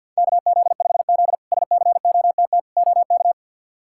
Morse code, heard as a keyed tone, about 52 wpm, 700 Hz. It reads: KZ5Z SCOTT OK
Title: K1USN SST QSO Practice - 50wpm